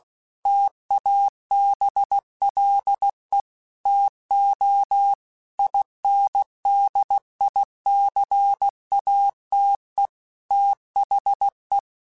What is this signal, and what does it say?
Morse code, heard as a keyed tone, about 16 wpm, 800 Hz. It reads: TABLE TO INDICATE THE
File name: Technical Subjects / Technical Subjects-mp3s / Gates Intro @15